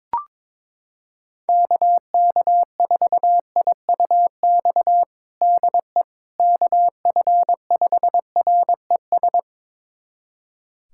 Morse code, heard as a keyed tone, about 22 wpm, 700 Hz. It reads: KK4IUX DE KF5RES
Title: USA Callsign DE Callsign 22wpm